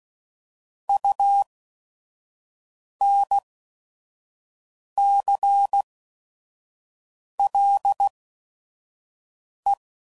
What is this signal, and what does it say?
Morse code, heard as a keyed tone, about 16 wpm, spaced as for 5 wpm, 800 Hz. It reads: UNCLE
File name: Prefixes / By First Letter / Us